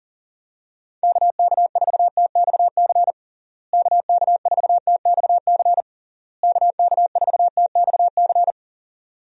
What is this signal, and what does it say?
Morse code, heard as a keyed tone, about 40 wpm, 700 Hz. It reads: KK4TXC KK4TXC KK4TXC
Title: USA Call Signs - 6 Characters - Repeated 3x - 40wpm